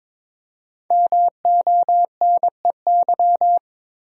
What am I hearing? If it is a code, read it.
Morse code, heard as a keyed tone, about 22 wpm, 700 Hz. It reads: MONEY
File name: 5 Letter Words - First 3 Letters Given - 22wpm